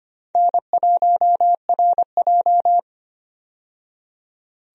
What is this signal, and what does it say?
Morse code, heard as a keyed tone, about 25 wpm, 700 Hz. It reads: N1RJ
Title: USA Call Signs - 4 Characters - 25wpm